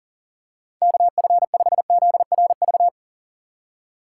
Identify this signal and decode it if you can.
Morse code, heard as a keyed tone, about 40 wpm, 700 Hz. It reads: KF5ZRV
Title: USA Call Signs - 6 Characters - 2x Character Spacing - 40wpm